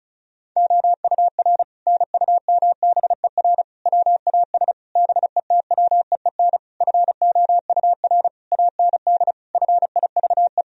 Morse code, heard as a keyed tone, about 34 wpm, 700 Hz. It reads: OUR NUMBER WAS BETWEEN FOUR AND FIVE
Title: Sentences from Top 1000 Words - Review Part 2 - 35wpm